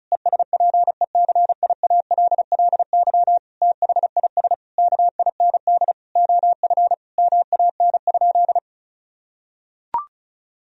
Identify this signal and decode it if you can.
Morse code, heard as a keyed tone, about 34 wpm, 700 Hz. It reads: ESPECIALLY THIS KIND OF MAN?
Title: Sentences from Top 1000 Words - Review Part 4 - 35wpm